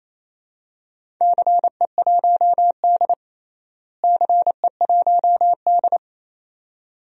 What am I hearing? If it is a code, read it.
Morse code, heard as a keyed tone, about 27 wpm, 700 Hz. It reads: CE1D CE1D
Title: DXCC-100 Call Signs - 4 Characters - Repeated 2x - 28wpm